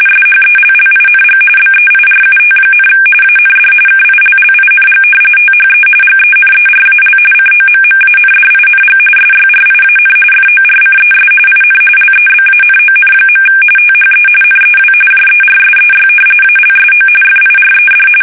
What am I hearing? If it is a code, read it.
RTTY, about 45.45 baud. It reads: SHELLING BEIRUT, AND KILLING THE GENERAL COMMANDING SYRIAN FORCES IN LEBANON. THE NEW JERSEY ALSO FIRED
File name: fsk850 / battleshipnj